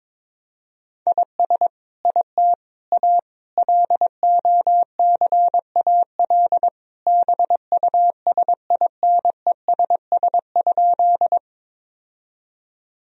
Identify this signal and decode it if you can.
Morse code, heard as a keyed tone, about 22 wpm, 700 Hz. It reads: IS IT A LOCAL BUSINESS?
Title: Sentences from Top 1000 Words - Review Part 3 - 22wpm